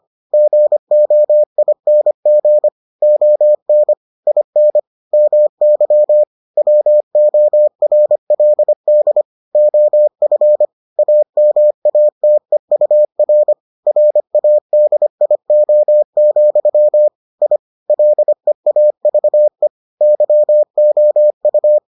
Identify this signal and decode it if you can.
Morse code, heard as a keyed tone, about 25 wpm, 600 Hz. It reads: GOING ON IN MY WORLD OF AMATEUR RADIO, I LEAVE YOU